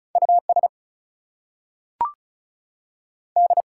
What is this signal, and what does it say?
Morse code, heard as a keyed tone, about 36 wpm, 700 Hz. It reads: US D
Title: Two Letter Words - 35wpm